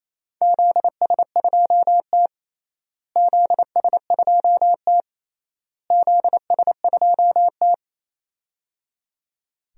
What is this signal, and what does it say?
Morse code, heard as a keyed tone, about 28 wpm, 700 Hz. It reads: ZS2T ZS2T ZS2T
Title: DXCC-100 Call Signs - 4 Characters - Repeated 3x - 28wpm